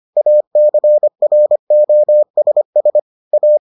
Morse code, heard as a keyed tone, about 25 wpm, 600 Hz. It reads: ACROSS A